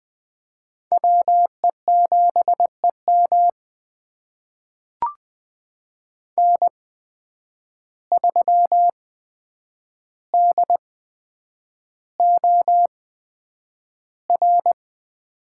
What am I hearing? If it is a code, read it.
Morse code, heard as a keyed tone, about 20 wpm, spaced as for 9 wpm, 700 Hz. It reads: WE7EM N3DOR
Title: USA Call Signs - 5 Characters - 8x Character Spacing - 20wpm